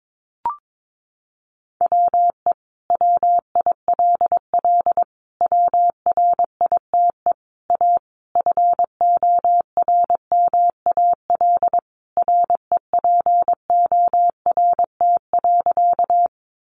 Morse code, heard as a keyed tone, about 22 wpm, 700 Hz. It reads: WE WILL WRITE A FORMAL REPORT.
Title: Sentences from Top 1300 Words 22wpm